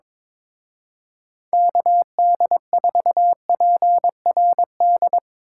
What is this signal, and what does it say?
Morse code, heard as a keyed tone, about 22 wpm, 700 Hz. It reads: KD4PRD
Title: Speed Racing - USA 2x3 Call Signs 22wpm